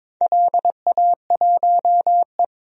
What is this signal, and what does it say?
Morse code, heard as a keyed tone, about 22 wpm, 700 Hz. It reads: LA1E
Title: DXCC-100 Call Signs - 4 Characters - Repeated 3x - 22wpm